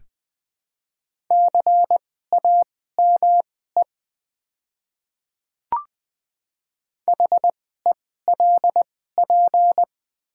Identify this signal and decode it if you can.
Morse code, heard as a keyed tone, about 20 wpm, spaced as for 15 wpm, 700 Hz. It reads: CAME HELP
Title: 2x Character Spacing - Top 200 Words - 20wpm